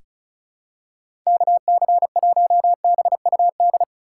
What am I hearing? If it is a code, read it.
Morse code, heard as a keyed tone, about 36 wpm, 700 Hz. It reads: KC1BUD
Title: USA Call Signs - 6 Characters - 2x Character Spacing - 35wpm